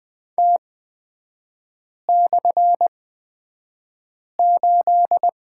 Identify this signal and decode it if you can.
Morse code, heard as a keyed tone, about 20 wpm, spaced as for 5 wpm, 700 Hz. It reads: T/8
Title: 250 Lesson 40e - BK - Three Characters 20wpm